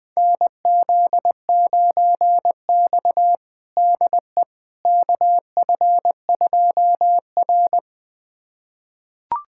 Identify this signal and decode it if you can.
Morse code, heard as a keyed tone, about 20 wpm, 700 Hz. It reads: NZ9X DE KF2R
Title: USA Callsign DE Callsign - 4 Characters - 20wpm